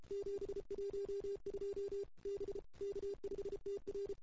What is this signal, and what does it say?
Morse code, heard as a keyed tone, about 30 wpm, 410 Hz. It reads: 712 B K5TR
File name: CW / 2008-ss-cw-k5tr-op-n5rz -04:00